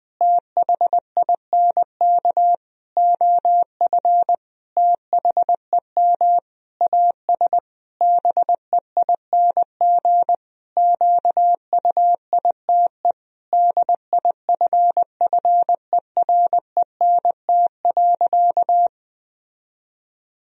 Morse code, heard as a keyed tone, about 20 wpm, 700 Hz. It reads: THINK OF THEM AS BEING QUITE DIFFERENT.
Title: Sentences from Top 1000 Words - Review Part 3 - 20wpm